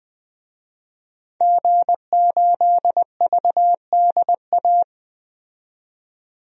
Morse code, heard as a keyed tone, about 20 wpm, 700 Hz. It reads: G8VDA
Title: DXCC-100 Call Signs - 5 Characters - 20wpm